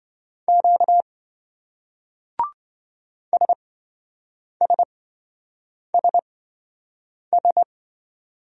Morse code, heard as a keyed tone, about 28 wpm, spaced as for 7 wpm, 700 Hz. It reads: Q SSSS